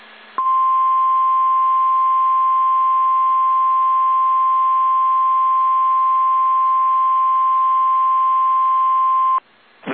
Emergency Alert System (SAME) tones (the 1050 Hz attention tone). It is heard from 0.4 to 9.4 s.
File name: eas / KSDZKDJL / KSDZKDJL